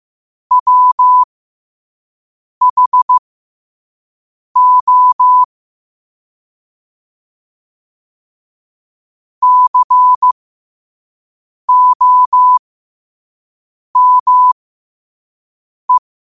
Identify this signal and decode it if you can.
Morse code, heard as a keyed tone, about 15 wpm, spaced as for 5 wpm, 1000 Hz. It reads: WHO COME
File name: qotdfiles / 5wpm